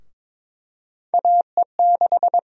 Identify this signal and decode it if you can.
Morse code, heard as a keyed tone, about 22 wpm, 700 Hz. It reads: AE6
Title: USA Call Sign Prefixes - 22wpm